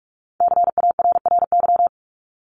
Morse code, heard as a keyed tone, about 46 wpm, 700 Hz. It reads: CARRY
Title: Top 300 Words 45wpm